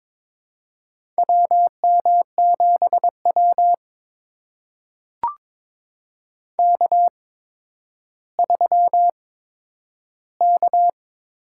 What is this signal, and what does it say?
Morse code, heard as a keyed tone, about 22 wpm, spaced as for 11 wpm, 700 Hz. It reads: WM7W K3K
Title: USA Call Signs - 4 Characters - 8x Character Spacing - 22wpm